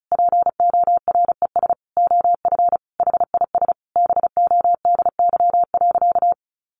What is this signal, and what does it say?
Morse code, heard as a keyed tone, about 34 wpm, 700 Hz. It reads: PORES OF HIS BODY.
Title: A Sound Of Thunder 35wpm